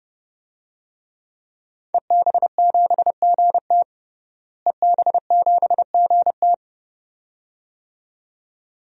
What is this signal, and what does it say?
Morse code, heard as a keyed tone, about 30 wpm, 700 Hz. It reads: EB7GT EB7GT
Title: DXCC-100 Call Signs - 5 Characters - Repeated 2x - 30wpm